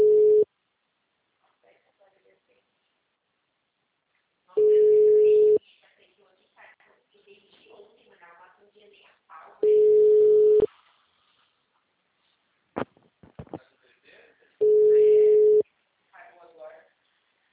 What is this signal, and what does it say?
Ringback tone (Europe, 425 Hz), with speech.
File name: out- -2017- - - .99856